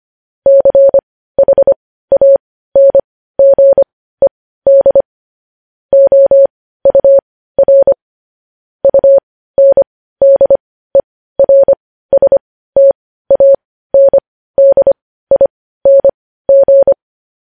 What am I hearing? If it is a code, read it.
Morse code, heard as a keyed tone, about 25 wpm, spaced as for 15 wpm, 550 Hz. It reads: CHANGED OUR UNDERSTANDING
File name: podcast / News headlines-1x-25x15